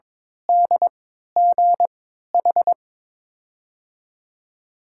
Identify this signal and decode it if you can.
Morse code, heard as a keyed tone, about 23 wpm, spaced as for 13 wpm, 700 Hz. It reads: DGH
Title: USA Call Signs - 3x Character Spacing - 22wpm